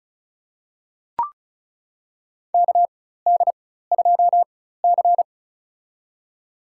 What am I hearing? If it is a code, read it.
Morse code, heard as a keyed tone, about 36 wpm, spaced as for 16 wpm, 700 Hz. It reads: KD2C
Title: USA Call Signs - 4x Character Spacing - 35wpm